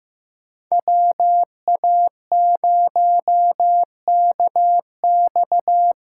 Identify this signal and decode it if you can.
Morse code, heard as a keyed tone, about 15 wpm, 700 Hz. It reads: WA0KX
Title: USA Call Signs - 6 Characters - 8x Character Spacing - 15wpm